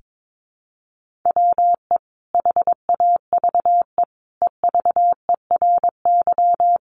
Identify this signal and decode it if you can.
Morse code, heard as a keyed tone, about 22 wpm, 700 Hz. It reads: WE HAVE EVERY
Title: Sentences from Top 1100 Words 22wpm